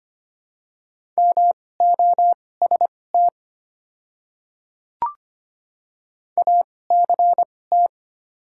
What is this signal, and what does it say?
Morse code, heard as a keyed tone, about 25 wpm, spaced as for 18 wpm, 700 Hz. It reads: MOST ACT
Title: 2x Character Spacing - Top 200 Words - 25wpm